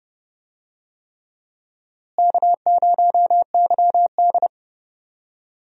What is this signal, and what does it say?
Morse code, heard as a keyed tone, about 29 wpm, 700 Hz. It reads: K0YD
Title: USA Call Signs - 4 Characters - Repeated 3x - 30wpm